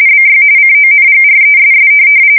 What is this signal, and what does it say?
RTTY, about 45.45 baud. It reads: BY CONGRESS
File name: fsk170 / USS-Missouri